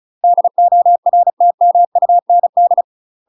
Morse code, heard as a keyed tone, about 35 wpm, 700 Hz. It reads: DORTMUND
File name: UK MASTER QSOs TRAINING / 35WPM / UK MASTER QSOs 0017